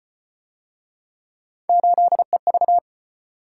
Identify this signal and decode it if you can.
Morse code, heard as a keyed tone, about 32 wpm, 700 Hz. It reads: 8EV